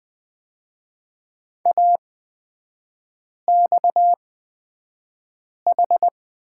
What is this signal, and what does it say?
Morse code, heard as a keyed tone, about 20 wpm, spaced as for 5 wpm, 700 Hz. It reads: AXH
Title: 236 Lesson 38e - Letter q - Three Characters 20wpm